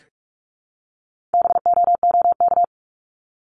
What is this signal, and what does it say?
Morse code, heard as a keyed tone, about 44 wpm, 700 Hz. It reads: BOOK
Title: ES:S Suffix - Top 100 Words with Base Words 45wpm